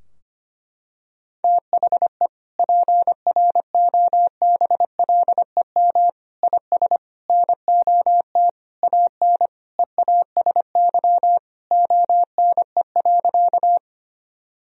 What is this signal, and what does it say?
Morse code, heard as a keyed tone, about 25 wpm, 700 Hz. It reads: THE PROBLEM IS NOT AN EASY ONE.
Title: Sentences from Top 1000 Words - Review Part 2 - 25wpm